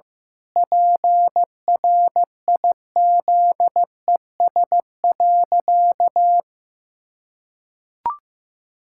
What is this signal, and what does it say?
Morse code, heard as a keyed tone, about 15 wpm, 700 Hz. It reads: PRIZES.
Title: Pangram Challenge v2 - 15wpm